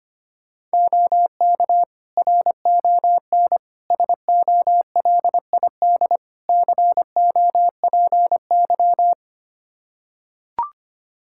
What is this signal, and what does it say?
Morse code, heard as a keyed tone, about 25 wpm, 700 Hz. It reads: OK RON SOLID COPY
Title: Practice QSO - Exchange 3 - More Info - 25wpm